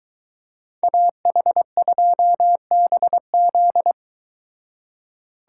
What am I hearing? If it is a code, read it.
Morse code, heard as a keyed tone, about 22 wpm, 700 Hz. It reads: AH2BZ